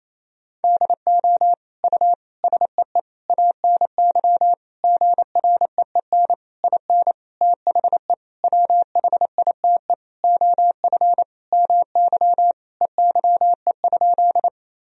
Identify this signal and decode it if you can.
Morse code, heard as a keyed tone, about 28 wpm, 700 Hz. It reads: DO U SEE ANY GREEN IN THE WHITE OF MY EYE?
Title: Sentences from Top 500 Words and Abbreviations 28wpm